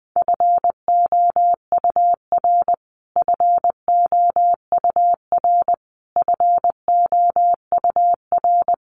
Morse code, heard as a keyed tone, about 20 wpm, 700 Hz. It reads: FOUR FOUR FOUR
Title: Top 500 Words Repeated 20wpm